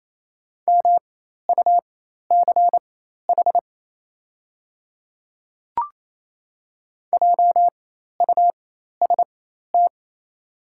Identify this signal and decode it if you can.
Morse code, heard as a keyed tone, about 28 wpm, spaced as for 13 wpm, 690 Hz. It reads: MUCH JUST
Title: 4x Character Spacing - Top 200 Words - 28wpm